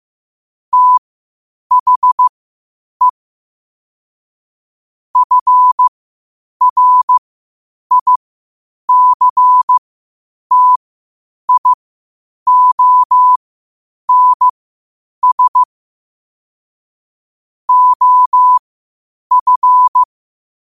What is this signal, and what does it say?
Morse code, heard as a keyed tone, about 15 wpm, spaced as for 8 wpm, 1000 Hz. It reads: THE FRICTIONS OF
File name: qotdfiles / 8wpm